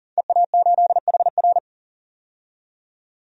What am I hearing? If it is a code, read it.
Morse code, heard as a keyed tone, about 40 wpm, 700 Hz. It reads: EA8HR
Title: DXCC-100 Call Signs - 5 Characters - 40wpm